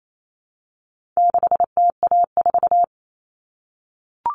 Morse code, heard as a keyed tone, about 28 wpm, 700 Hz. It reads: 6TA4
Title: All Cut Numbers - 4-Digit 28wpm